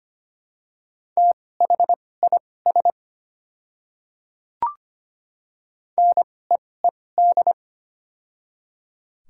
Morse code, heard as a keyed tone, about 25 wpm, spaced as for 18 wpm, 700 Hz. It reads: THIS NEED
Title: 2x Character Spacing - Top 200 Words - 25wpm